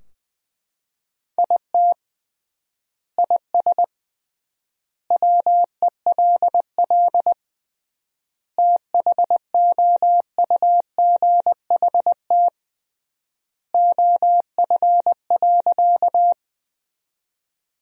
Morse code, heard as a keyed tone, about 20 wpm, 700 Hz. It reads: IT IS WELL THOUGHT OF.
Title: 3x Word-Spacing - Sentences from Top 300 Words - 20wpm